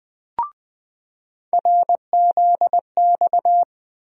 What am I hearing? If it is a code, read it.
Morse code, heard as a keyed tone, about 20 wpm, 700 Hz. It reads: RZX
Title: USA Call Sign Postfixes - 3 Letters 20wpm